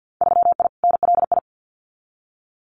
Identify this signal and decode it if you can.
Morse code, heard as a keyed tone, about 50 wpm, 700 Hz. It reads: 3I NLI
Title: Winter Field Day Exchanges Encore 50wpm